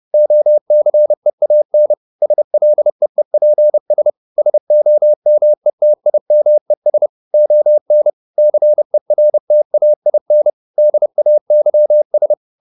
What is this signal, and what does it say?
Morse code, heard as a keyed tone, about 30 wpm, 600 Hz. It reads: OCEAN SLEEPS SOMETIMES ON CERTAIN DAYS